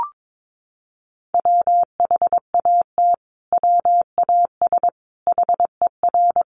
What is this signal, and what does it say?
Morse code, heard as a keyed tone, about 22 wpm, 700 Hz. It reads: WHAT WAS HER
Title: Sentences from Top 1400 Words 22wpm